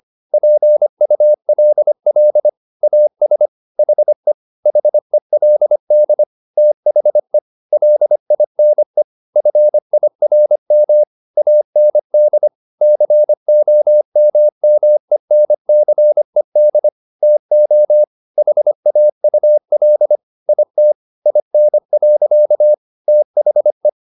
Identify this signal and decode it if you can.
Morse code, heard as a keyed tone, about 25 wpm, 600 Hz. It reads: PULL AS HE HELD THE LINE FIRM AND COMMENCED TO HAUL IT IN. THE
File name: the old man and the sea part 1 25wpm